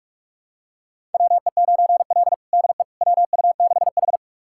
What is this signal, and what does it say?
Morse code, heard as a keyed tone, about 42 wpm, 700 Hz. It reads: WE9L DE WU6H
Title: USA Callsign DE Callsign - 4 Characters - 45wpm